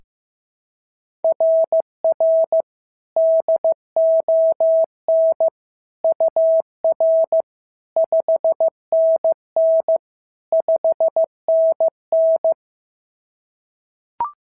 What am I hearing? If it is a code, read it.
Morse code, heard as a keyed tone, about 14 wpm, 650 Hz. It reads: RR DON UR 5NN 5NN
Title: SOTA QSO Practice 15wpm